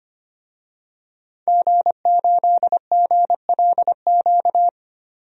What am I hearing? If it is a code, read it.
Morse code, heard as a keyed tone, about 25 wpm, 700 Hz. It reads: G8GLQ
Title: DXCC-100 Call Signs - 5 Characters - 25wpm